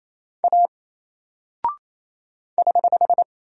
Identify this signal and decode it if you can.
Morse code, heard as a keyed tone, about 27 wpm, 700 Hz. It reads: A <HH>
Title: Single Character (Letter:Number:Punctuation:Prosign) 28wpm